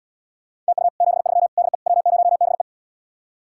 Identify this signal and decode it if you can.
Morse code, heard as a keyed tone, about 77 wpm, 700 Hz. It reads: TU 73 DE W0BE